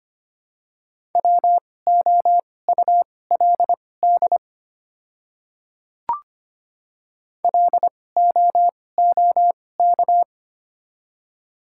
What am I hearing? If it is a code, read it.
Morse code, heard as a keyed tone, about 25 wpm, spaced as for 18 wpm, 700 Hz. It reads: WOULD LOOK
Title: 2x Character Spacing - Top 100 Words - 25wpm